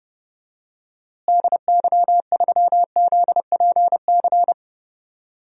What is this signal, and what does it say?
Morse code, heard as a keyed tone, about 32 wpm, 700 Hz. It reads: DY3ZPC
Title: DXCC-100 Call Signs - 6 Characters - Repeated 3x - 30wpm